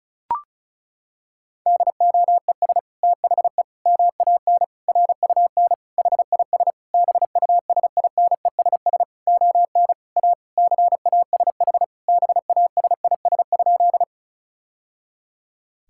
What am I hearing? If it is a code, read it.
Morse code, heard as a keyed tone, about 36 wpm, 700 Hz. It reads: DOES THE MAN RUN HIS BUSINESS ON A CASH BASIS?
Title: Sentences from Top 1000 Words - Review Part 5 - 35wpm